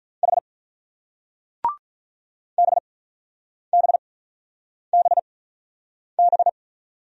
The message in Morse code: H B B B B

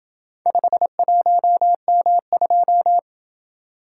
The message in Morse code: 51M2

